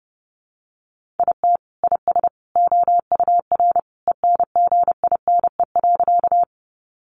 Morse code, heard as a keyed tone, about 30 wpm, 700 Hz. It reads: IT IS OUR ENGINE.
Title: Sentences from Top 1500 Words 30wpm